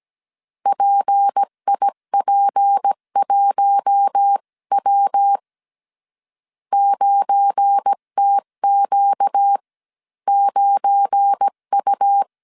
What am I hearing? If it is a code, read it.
Morse code, heard as a keyed tone, about 17 wpm, spaced as for 14 wpm, 800 Hz. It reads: PIP1W 9TQ9U
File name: 1500C S B 16wpm